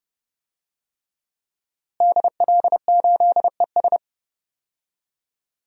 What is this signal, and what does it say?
Morse code, heard as a keyed tone, about 30 wpm, 700 Hz. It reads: DL8ES
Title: DXCC-100 Call Signs - 5 Characters - Repeated 3x - 30wpm